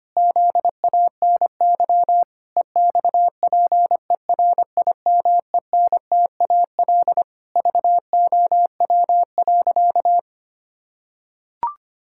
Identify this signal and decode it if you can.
Morse code, heard as a keyed tone, about 25 wpm, 700 Hz. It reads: ZANY EXPERIMENTAL VOW.
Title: Pangram Challenge v2 - 25wpm